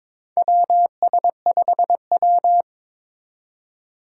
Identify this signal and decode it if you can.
Morse code, heard as a keyed tone, about 22 wpm, 700 Hz. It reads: WS5W
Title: Four Challenging Letter-Number Combinations 22wpm